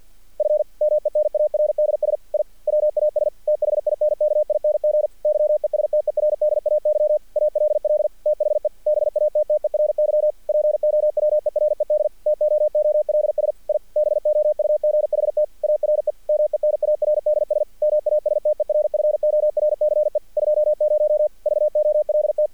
Morse code, heard as a keyed tone, about 48 wpm, 600 Hz. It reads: F MENARDS I WAS THINKING YESTERDAY ALL THE BATTERY POWERED TOOLS I BOUGHT ARE MENARDS MASTERFORCE 20 VOLT
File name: archived / CRSnet